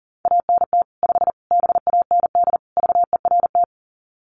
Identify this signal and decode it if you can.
Morse code, heard as a keyed tone, about 37 wpm, 700 Hz. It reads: ANT 5 BAND VERT
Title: Common QSO Phrases 40wpm